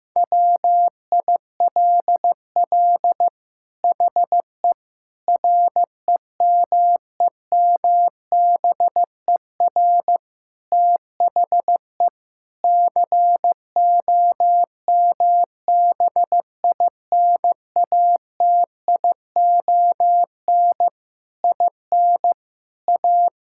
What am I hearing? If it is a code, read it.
Morse code, heard as a keyed tone, about 16 wpm, 700 Hz. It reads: WILL HE REMEMBER THE COMBINATION IN A
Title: Sentences from Top 1000 Words - Review Part 6 - 15wpm